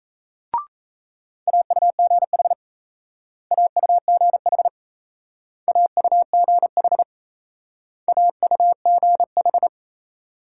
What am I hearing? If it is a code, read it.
Morse code, heard as a keyed tone, about 36 wpm, 700 Hz. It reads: AUGH AUGH AUGH AUGH